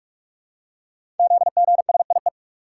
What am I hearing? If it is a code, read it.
Morse code, heard as a keyed tone, about 44 wpm, 700 Hz. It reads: ZGSIE
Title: Code Groups - 5 Letters - 45wpm